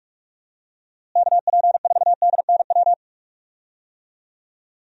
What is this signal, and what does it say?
Morse code, heard as a keyed tone, about 46 wpm, 700 Hz. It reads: KP4DNW
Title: USA Call Signs - 6 Characters - Repeated 3x - 45wpm